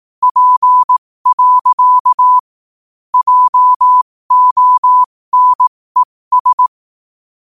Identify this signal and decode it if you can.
Morse code, heard as a keyed tone, about 18 wpm, spaced as for 15 wpm, 1000 Hz. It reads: P. JONES
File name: qotdfiles / 15wpm